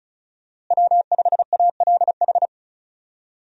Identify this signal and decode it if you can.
Morse code, heard as a keyed tone, about 36 wpm, 700 Hz. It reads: W5ALH
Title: USA Call Signs - 5 Characters - Repeated 3x - 35wpm